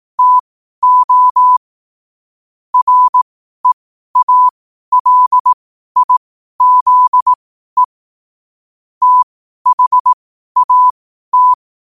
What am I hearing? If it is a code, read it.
Morse code, heard as a keyed tone, about 18 wpm, spaced as for 12 wpm, 1000 Hz. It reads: TO REALIZE THAT
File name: qotdfiles / 12wpm